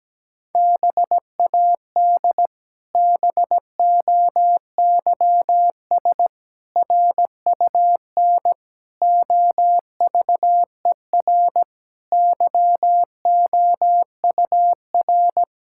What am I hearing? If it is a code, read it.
Morse code, heard as a keyed tone, about 17 wpm, 700 Hz. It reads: BAD BOYS RUN OVER YOUR
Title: Pangram Challenge v2 - 17wpm